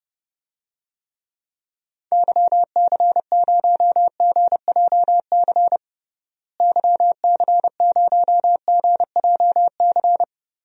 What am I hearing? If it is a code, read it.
Morse code, heard as a keyed tone, about 30 wpm, 700 Hz. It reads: YC0GJC YC0GJC
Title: DXCC-100 Call Signs - 6 Characters - Repeated 3x - 30wpm